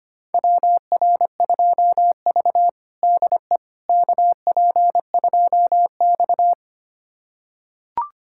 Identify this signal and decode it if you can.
Morse code, heard as a keyed tone, about 25 wpm, 700 Hz. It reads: WR2V DE KP2X
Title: USA Callsign DE Callsign - 4 Characters - 25wpm